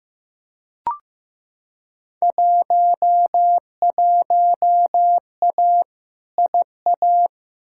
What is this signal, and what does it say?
Morse code, heard as a keyed tone, about 15 wpm, 700 Hz. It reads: 11A IA
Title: Field Day Exchanges Encore - 15wpm